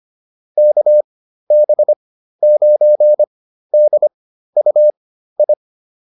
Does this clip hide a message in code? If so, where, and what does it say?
Yes: Morse code from 0.6 to 5.5 s, heard as a keyed tone, about 25 wpm, spaced as for 13 wpm, 600 Hz. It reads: KB9DUI